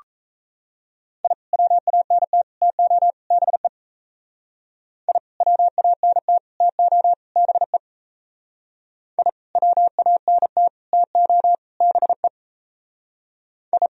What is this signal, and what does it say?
Morse code, heard as a keyed tone, about 37 wpm, 700 Hz. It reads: I WANT TO BE I WANT TO BE I WANT TO BE I